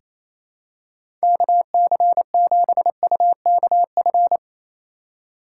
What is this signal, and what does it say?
Morse code, heard as a keyed tone, about 28 wpm, 700 Hz. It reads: KC7UKF
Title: USA Call Signs - 6 Characters - 2x Character Spacing - 28wpm